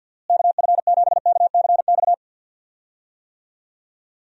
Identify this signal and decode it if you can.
Morse code, heard as a keyed tone, about 47 wpm, 700 Hz. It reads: KF7KCX
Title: USA Call Signs - 6 Characters - Repeated 3x - 50wpm